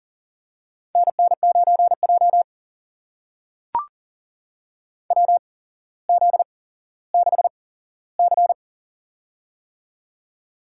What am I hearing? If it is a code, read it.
Morse code, heard as a keyed tone, about 39 wpm, spaced as for 17 wpm, 700 Hz. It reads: NN9J WZ6C
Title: USA Call Signs - 4 Characters - 8x Character Spacing - 40wpm